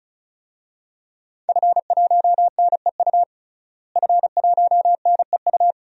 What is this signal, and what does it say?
Morse code, heard as a keyed tone, about 36 wpm, 700 Hz. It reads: F1NEU F1NEU
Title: DXCC-100 Call Signs - 5 Characters - Repeated 2x - 35wpm